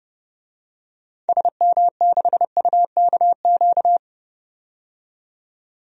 Morse code, heard as a keyed tone, about 30 wpm, 700 Hz. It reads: SM6UKQ
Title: DXCC-100 Call Signs - 6 Characters - 30wpm